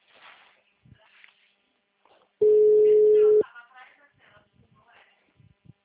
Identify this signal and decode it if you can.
Ringback tone (Europe, 425 Hz), with speech.